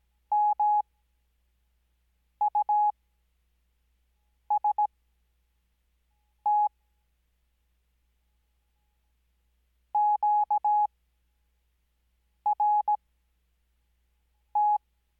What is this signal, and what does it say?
Morse code, heard as a keyed tone, about 17 wpm, spaced as for 5 wpm, 840 Hz. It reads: MUST QRT
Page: Farnsworth Spacing: 15WPM Character speed Spaced at 5WPM.